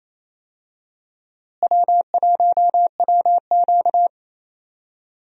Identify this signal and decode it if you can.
Morse code, heard as a keyed tone, about 28 wpm, 700 Hz. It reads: W1WQ
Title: USA Call Signs - 4 Characters - Repeated 2x - 28wpm